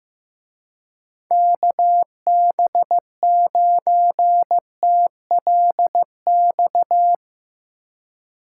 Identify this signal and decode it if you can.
Morse code, heard as a keyed tone, about 16 wpm, 700 Hz. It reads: KB9TLX
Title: Speed Racing - USA 2x3 Call Signs 15wpm